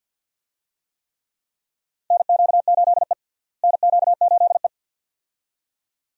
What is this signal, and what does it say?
Morse code, heard as a keyed tone, about 46 wpm, 700 Hz. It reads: NQ8E NQ8E
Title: USA Call Signs - 4 Characters - Repeated 2x - 50wpm